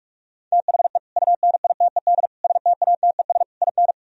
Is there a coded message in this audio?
Yes: Morse code from 0.5 to 3.9 s, heard as a keyed tone, about 48 wpm, 700 Hz. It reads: THE UNITED STATES IN